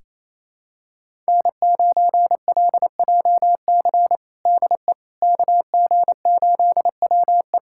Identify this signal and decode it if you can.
Morse code, heard as a keyed tone, about 28 wpm, 700 Hz. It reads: N9LJC DE KG8WE
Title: USA Callsign DE Callsign - 5 Characters - 28wpm